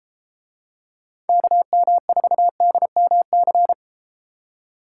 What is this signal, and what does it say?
Morse code, heard as a keyed tone, about 32 wpm, 700 Hz. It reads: KM4DMC